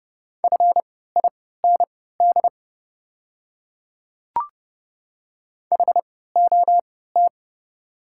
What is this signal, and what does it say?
Morse code, heard as a keyed tone, about 30 wpm, spaced as for 17 wpm, 700 Hz. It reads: FIND HOT
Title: 3x Character Spacing - Top 100 Words - 30wpm